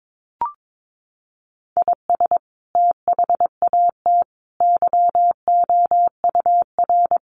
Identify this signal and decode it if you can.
Morse code, heard as a keyed tone, about 22 wpm, 700 Hz. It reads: IS THAT YOUR
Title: Sentences from Top 1200 Words 22wpm